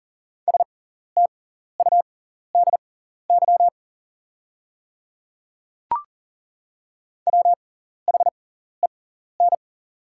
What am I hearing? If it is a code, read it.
Morse code, heard as a keyed tone, about 40 wpm, spaced as for 14 wpm, 690 Hz. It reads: STUDY WHEN
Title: 6x Character Spacing - Top 300 Words - 40wpm